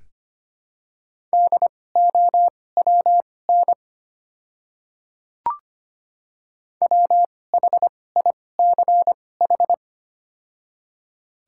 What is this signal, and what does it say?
Morse code, heard as a keyed tone, about 25 wpm, spaced as for 18 wpm, 700 Hz. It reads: DOWN WHICH
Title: 2x Character Spacing - Top 100 Words - 25wpm